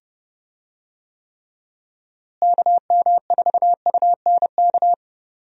Morse code, heard as a keyed tone, about 30 wpm, 700 Hz. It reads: KM4UNK
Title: USA Call Signs - 6 Characters - Repeated 2x - 30wpm